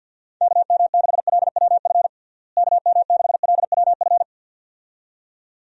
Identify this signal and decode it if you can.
Morse code, heard as a keyed tone, about 48 wpm, 700 Hz. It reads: KM6LWF KM6LWF
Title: USA Call Signs - 6 Characters - Repeated 2x - 50wpm